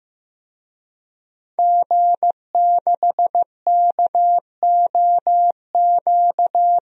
Morse code, heard as a keyed tone, about 15 wpm, 700 Hz. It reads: G6KOQ
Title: DXCC-100 Call Signs - 5 Characters - Repeated 2x - 15wpm